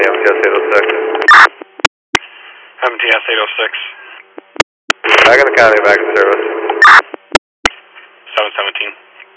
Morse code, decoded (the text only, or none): none